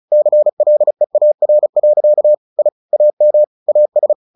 Morse code, heard as a keyed tone, about 34 wpm, 600 Hz. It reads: CLEAR. I AM AS
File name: the old man and the sea 2 35wpm